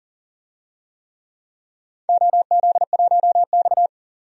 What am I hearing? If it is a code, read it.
Morse code, heard as a keyed tone, about 38 wpm, 700 Hz. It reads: OZ1X
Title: DXCC-100 Call Signs - 4 Characters - 40wpm